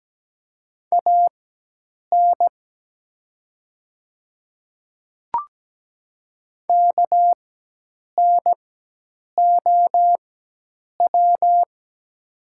Morse code, heard as a keyed tone, about 17 wpm, spaced as for 8 wpm, 700 Hz. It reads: AN KNOW
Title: 4x Character Spacing - Top 100 Words - 17wpm